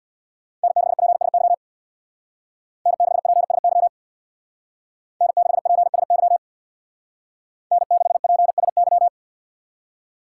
Morse code, heard as a keyed tone, about 57 wpm, 700 Hz. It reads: N6PSY N6PSY N6PSY N6PSY
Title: Speed Racing - USA 1x3 Call Signs 50wpm